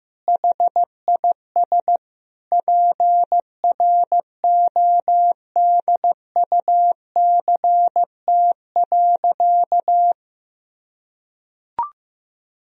Morse code, heard as a keyed tone, about 16 wpm, 700 Hz. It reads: HIS PRODUCT.
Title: Sentences from Top 1000 Words - Review Part 2 - 15wpm